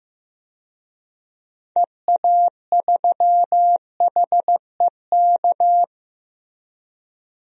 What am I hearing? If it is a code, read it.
Morse code, heard as a keyed tone, about 15 wpm, 700 Hz. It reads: EA3HEK
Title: DXCC-100 Call Signs - 6 Characters - 15wpm